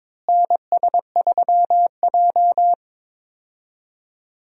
Morse code, heard as a keyed tone, about 22 wpm, 700 Hz. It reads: NS3J
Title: USA Call Signs - 4 Characters - Repeated 3x - 22wpm